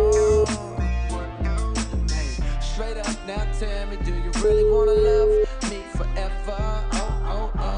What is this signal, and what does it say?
Ringback tone (Europe, 425 Hz), with noise.